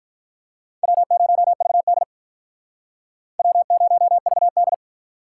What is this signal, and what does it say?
Morse code, heard as a keyed tone, about 47 wpm, 700 Hz. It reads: W0VD W0VD